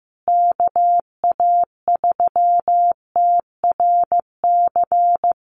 Morse code, heard as a keyed tone, about 15 wpm, 700 Hz. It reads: KA3TRC
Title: USA Call Signs Repeated 3x 15wpm